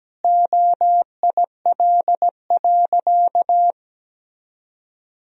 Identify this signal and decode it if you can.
Morse code, heard as a keyed tone, about 17 wpm, 700 Hz. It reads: OIL.
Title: Sentences from Top 1000 Words - Review Part 4 - 17wpm